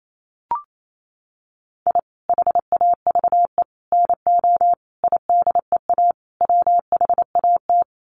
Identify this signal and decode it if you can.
Morse code, heard as a keyed tone, about 28 wpm, 700 Hz. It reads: I HAVE NO IDEA WHAT
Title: Sets of 5 Words Encore 28wpm